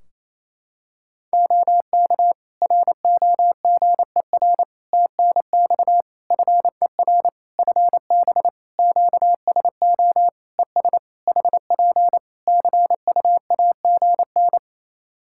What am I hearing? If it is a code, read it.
Morse code, heard as a keyed tone, about 28 wpm, 700 Hz. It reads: OK ROGER TNX FER FB QSO ES HP CUAGN
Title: Practice QSO - Exchange 4 - Ending - 28wpm